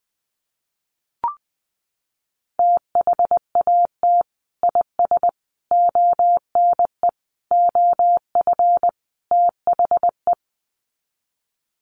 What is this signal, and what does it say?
Morse code, heard as a keyed tone, about 20 wpm, 700 Hz. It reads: THAT IS ONE OF THE
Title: Sets of 5 Words Encore 20wpm